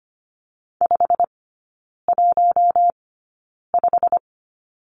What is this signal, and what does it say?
Morse code, heard as a keyed tone, about 25 wpm, spaced as for 9 wpm, 690 Hz. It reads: 515
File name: Lesson 15e - Number 5 - Three Characters 25wpm